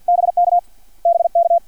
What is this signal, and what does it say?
Morse code, heard as a keyed tone, about 50 wpm, 680 Hz, with radio noise. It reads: BK BK
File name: archived / CRSnet